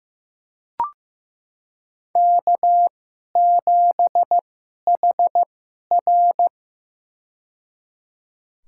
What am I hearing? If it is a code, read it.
Morse code, heard as a keyed tone, about 15 wpm, spaced as for 11 wpm, 700 Hz. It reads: K7HR
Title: USA Call Signs - 4 Characters - 2x Character Spacing - 15wpm